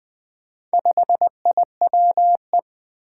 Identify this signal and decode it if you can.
Morse code, heard as a keyed tone, about 20 wpm, 700 Hz. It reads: 5IWE
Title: Four Challenging Letter-Number Combinations 20wpm